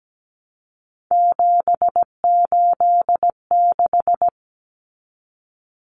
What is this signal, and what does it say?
Morse code, heard as a keyed tone, about 17 wpm, 700 Hz. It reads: 786
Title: Common Cut Numbers - 3-Digit 17wpm